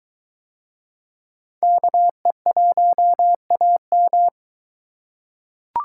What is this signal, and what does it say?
Morse code, heard as a keyed tone, about 23 wpm, 700 Hz. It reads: KE1AM